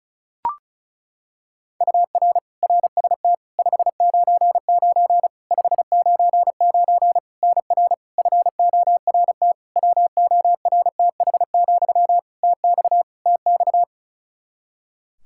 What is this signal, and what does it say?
Morse code, heard as a keyed tone, about 36 wpm, 700 Hz. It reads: UR RST 599 599 NR FORT WORTH, TX TX
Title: Practice QSO - Exchange 1 - Beginning - 35wpm